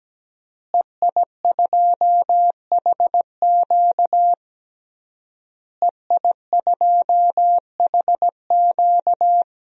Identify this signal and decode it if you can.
Morse code, heard as a keyed tone, about 17 wpm, 700 Hz. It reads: EI2HQ EI2HQ
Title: DXCC-100 Call Signs - 5 Characters - Repeated 3x - 17wpm